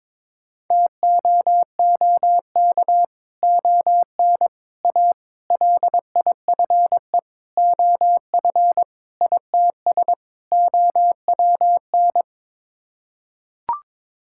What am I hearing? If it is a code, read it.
Morse code, heard as a keyed tone, about 22 wpm, 700 Hz. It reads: TOOK ON A LIFE OF ITS OWN
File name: practice / 22wpm / Sets of 7 Words - 22wpm